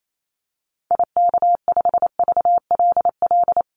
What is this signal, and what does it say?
Morse code, heard as a keyed tone, about 28 wpm, 700 Hz. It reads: IK5VLL
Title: Call Signs - 28wpm